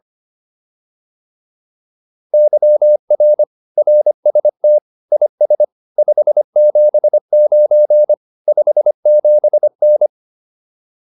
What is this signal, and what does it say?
Morse code, heard as a keyed tone, about 25 wpm, 600 Hz. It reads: YR RST IS 579 57N
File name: QSO304a 25